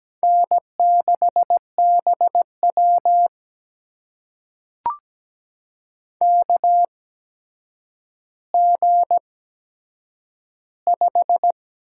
Morse code, heard as a keyed tone, about 17 wpm, spaced as for 8 wpm, 700 Hz. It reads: N6BW KG5
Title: USA Call Signs - 4 Characters - 8x Character Spacing - 17wpm